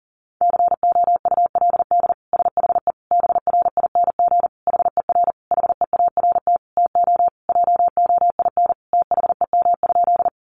Morse code, heard as a keyed tone, about 40 wpm, 700 Hz. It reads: COULD SHE BRING HER HEART TO JOIN THEM?
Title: Sentences from Top 1400 Words 40wpm